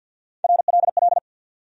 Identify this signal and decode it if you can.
Morse code, heard as a keyed tone, about 47 wpm, 700 Hz. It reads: RLL
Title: USA Call Sign Postfixes - 3 Letters 50wpm